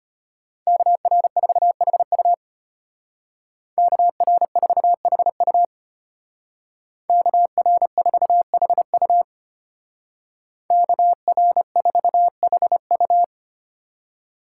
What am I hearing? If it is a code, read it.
Morse code, heard as a keyed tone, about 32 wpm, 700 Hz. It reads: KR4HU KR4HU KR4HU KR4HU